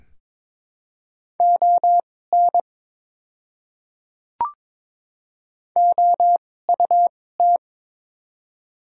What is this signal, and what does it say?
Morse code, heard as a keyed tone, about 22 wpm, spaced as for 16 wpm, 700 Hz. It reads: ON OUT
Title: 2x Character Spacing - Top 100 Words - 22wpm